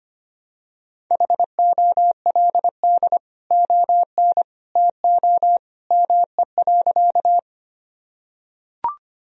Morse code, heard as a keyed tone, about 25 wpm, 700 Hz. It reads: HOLD ON TO ME.
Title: Sentences from Top 1000 Words - Review Part 2 - 25wpm